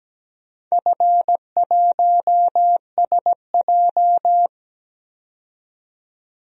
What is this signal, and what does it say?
Morse code, heard as a keyed tone, about 17 wpm, 700 Hz. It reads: F1SJ
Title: DXCC-100 Call Signs - 4 Characters - 17wpm